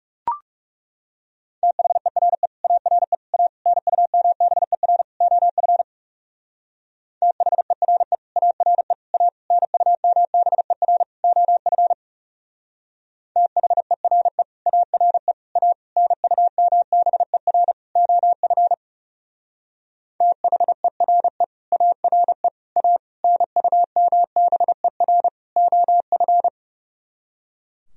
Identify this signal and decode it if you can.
Morse code, heard as a keyed tone, about 34 wpm, 700 Hz. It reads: THERE ARE A NUMBER OF THERE ARE A NUMBER OF THERE ARE A NUMBER OF THERE ARE A NUMBER OF
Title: Speed-Racing - Sets of 5 Words 30wpm